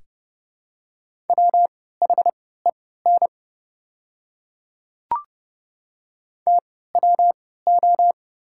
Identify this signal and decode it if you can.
Morse code, heard as a keyed tone, about 30 wpm, spaced as for 17 wpm, 700 Hz. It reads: WHEN TWO
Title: 3x Character Spacing - Top 100 Words - 30wpm